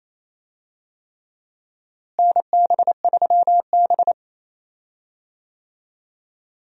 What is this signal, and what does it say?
Morse code, heard as a keyed tone, about 28 wpm, 700 Hz. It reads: NB3B
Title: USA Call Signs - 4 Characters - Repeated 3x - 28wpm